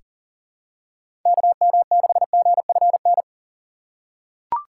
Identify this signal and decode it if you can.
Morse code, heard as a keyed tone, about 40 wpm, 700 Hz. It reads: KM6GFN
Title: USA Call Signs - 6 Characters - 3x Character Spacing - 40wpm